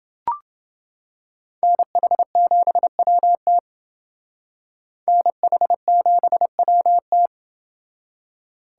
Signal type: Morse code

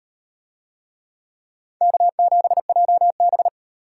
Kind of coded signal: Morse code

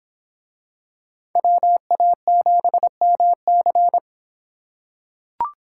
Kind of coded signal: Morse code

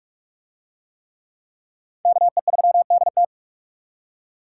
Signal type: Morse code

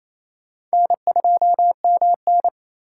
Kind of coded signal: Morse code

